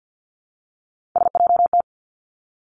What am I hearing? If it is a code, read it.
Morse code, heard as a keyed tone, about 50 wpm, 700 Hz. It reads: SJT
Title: Three Challenging Letter-Number Combinations 50wpm